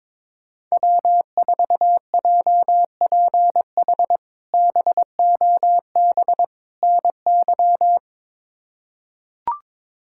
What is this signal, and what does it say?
Morse code, heard as a keyed tone, about 22 wpm, 700 Hz. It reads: W4JPH BOB NY
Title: K1USN SST QSO Practice - 22wpm